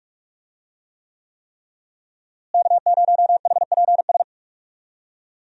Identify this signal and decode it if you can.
Morse code, heard as a keyed tone, about 46 wpm, 700 Hz. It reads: K0HPS
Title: USA Call Signs - 5 Characters - Repeated 3x - 45wpm